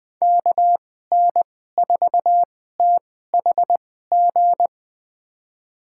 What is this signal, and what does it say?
Morse code, heard as a keyed tone, about 20 wpm, spaced as for 14 wpm, 700 Hz. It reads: KN4THG
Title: USA Call Signs - 6 Characters - 2x Character Spacing - 20wpm